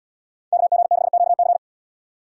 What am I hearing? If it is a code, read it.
Morse code, heard as a keyed tone, about 78 wpm, 690 Hz. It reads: KG6QC